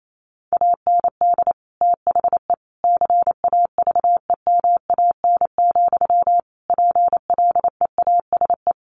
Morse code, heard as a keyed tone, about 28 wpm, 700 Hz. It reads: AND THE CAVEMAN, PLEASE
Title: A Sound Of Thunder 28wpm